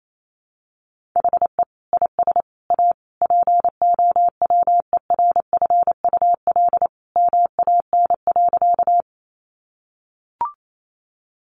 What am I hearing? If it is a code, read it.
Morse code, heard as a keyed tone, about 28 wpm, 700 Hz. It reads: HE IS A POWERFUL MAN.
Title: Sentences from Top 1300 Words 28wpm